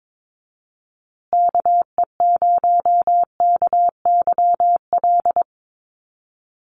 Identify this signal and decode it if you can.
Morse code, heard as a keyed tone, about 22 wpm, 700 Hz. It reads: KE0KYL
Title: USA Call Signs Repeated 3x 22wpm